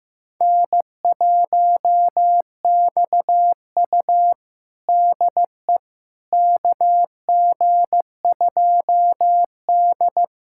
Morse code, heard as a keyed tone, about 15 wpm, 700 Hz. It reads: N1XU DE KG2D
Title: USA Callsign DE Callsign - 4 Characters - 15wpm